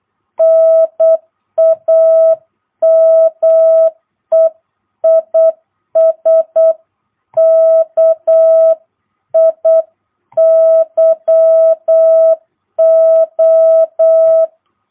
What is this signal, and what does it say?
Morse code, heard as a keyed tone, about 8 wpm, 660 Hz. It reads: NAME IS KIYO
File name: jh8cbh / 2022cw